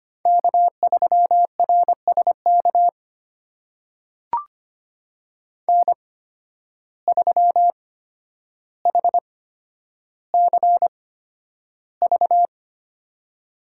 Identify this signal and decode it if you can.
Morse code, heard as a keyed tone, about 25 wpm, spaced as for 11 wpm, 700 Hz. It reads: K3RSK N3HCV
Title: USA Call Signs - 5 Characters - 8x Character Spacing - 25wpm